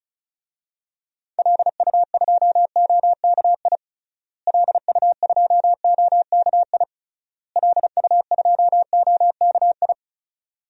Morse code, heard as a keyed tone, about 36 wpm, 700 Hz. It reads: LU2OKI LU2OKI LU2OKI
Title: DXCC-100 Call Signs - 6 Characters - Repeated 3x - 35wpm